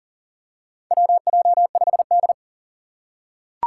Morse code, heard as a keyed tone, about 37 wpm, 700 Hz. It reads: WJ5D
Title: USA Call Signs - 4 Characters - 4x Character Spacing - 40wpm